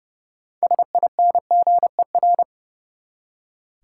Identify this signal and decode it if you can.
Morse code, heard as a keyed tone, about 30 wpm, 700 Hz. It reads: SINGER
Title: Six Letter Words - 30wpm